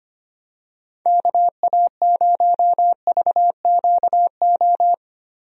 Morse code, heard as a keyed tone, about 25 wpm, 700 Hz. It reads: KA0VQO
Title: USA Call Signs - 6 Characters - 25wpm